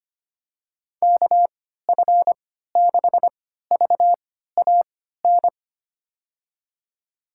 Morse code, heard as a keyed tone, about 25 wpm, spaced as for 14 wpm, 700 Hz. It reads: KF6VAN
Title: USA Call Signs - 6 Characters - 3x Character Spacing - 25wpm